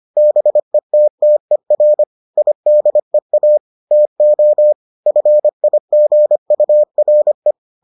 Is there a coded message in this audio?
Yes: Morse code from 0.2 to 7.5 s, heard as a keyed tone, about 25 wpm, 600 Hz. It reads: BETTER IDEA TO FIGURE